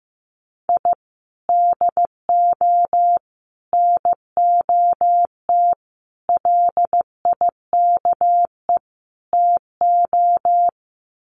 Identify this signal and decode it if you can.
Morse code, heard as a keyed tone, about 15 wpm, 700 Hz. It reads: I DO NOT LIKE TO
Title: Sets of 5 Words 15wpm